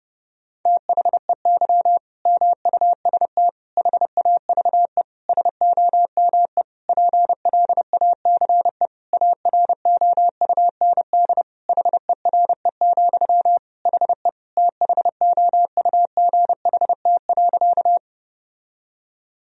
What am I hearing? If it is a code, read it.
Morse code, heard as a keyed tone, about 28 wpm, 700 Hz. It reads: THEY MUST HAVE SOME PLACE AROUND HERE, HE THOUGHT.
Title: Sentences from Top 1000 Words - Review Part 3 - 30wpm